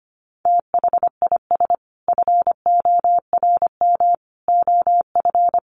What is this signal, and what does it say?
Morse code, heard as a keyed tone, about 25 wpm, 700 Hz. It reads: THIS FORM OF
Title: Sentences from Top 1100 Words 25wpm